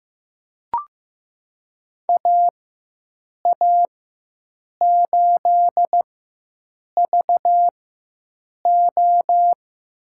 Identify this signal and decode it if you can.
Morse code, heard as a keyed tone, about 16 wpm, spaced as for 7 wpm, 700 Hz. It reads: AA8VO
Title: USA Call Signs - 5 Characters - 4x Character Spacing - 15wpm